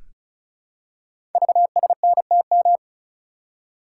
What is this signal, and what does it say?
Morse code, heard as a keyed tone, about 36 wpm, 700 Hz. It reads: VSNTM
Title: Code Groups - 5 Letter:Numbers - 35wpm